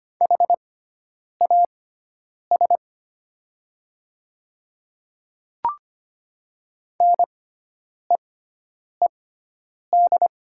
Morse code, heard as a keyed tone, about 25 wpm, spaced as for 9 wpm, 700 Hz. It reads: HAS NEED
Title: 6x Character Spacing - Top 200 Words - 25wpm